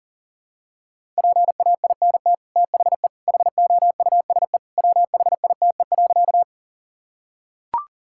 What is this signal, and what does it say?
Morse code, heard as a keyed tone, about 40 wpm, 700 Hz. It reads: PAINT THE HOUSE WHITE.
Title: Sentences from Top 1000 Words - Review Part 6 - 40wpm